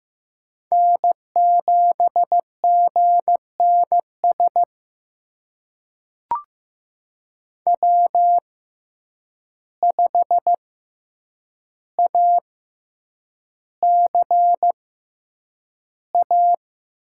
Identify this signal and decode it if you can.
Morse code, heard as a keyed tone, about 15 wpm, spaced as for 8 wpm, 700 Hz. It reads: N7GNS W5ACA
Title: USA Call Signs - 6x Character Spacing - 15wpm